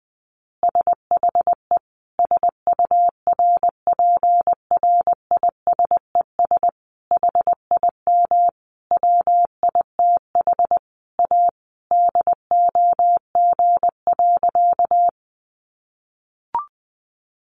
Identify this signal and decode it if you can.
Morse code, heard as a keyed tone, about 20 wpm, 700 Hz. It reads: SHE SURPRISES HIM WITH A DOG.
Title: Additional Easy Phrases 20wpm